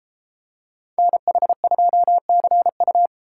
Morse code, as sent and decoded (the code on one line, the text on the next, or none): -. .... ..--- -.-. ..-
NH2CU